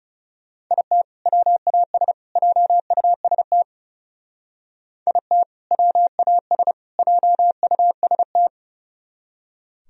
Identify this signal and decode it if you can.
Morse code, heard as a keyed tone, about 35 wpm, 700 Hz. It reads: IT WAS JUST IT WAS JUST
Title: Speed-Racing - Sets of 3 Words 30wpm